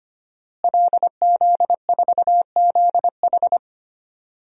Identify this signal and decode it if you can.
Morse code, heard as a keyed tone, about 25 wpm, 700 Hz. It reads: LZ4ZH
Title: DXCC-100 Call Signs - 5 Characters - Repeated 3x - 25wpm